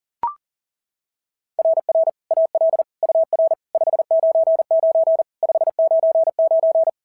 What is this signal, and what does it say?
Morse code, heard as a keyed tone, about 40 wpm, 690 Hz. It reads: RR AL UR 599 599
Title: SOTA QSO Practice 40wpm